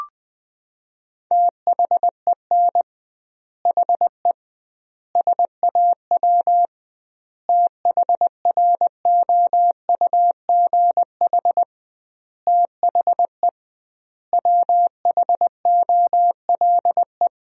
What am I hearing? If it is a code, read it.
Morse code, heard as a keyed tone, about 20 wpm, 700 Hz. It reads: THEN HE SAW THROUGH THE WHOLE
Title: 2x Word-Spacing - Sentences from Top 400 Words - 20wpm